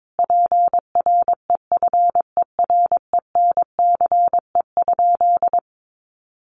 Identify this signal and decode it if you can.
Morse code, heard as a keyed tone, about 22 wpm, 700 Hz. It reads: PREFERENCE?
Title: Sentences from Top 1500 Words 22wpm